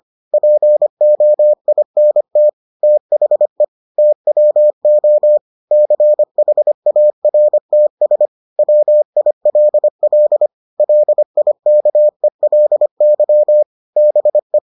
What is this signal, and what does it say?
Morse code, heard as a keyed tone, about 25 wpm, 600 Hz. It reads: POINT THE TWO CHARTS WILL LIKELY BE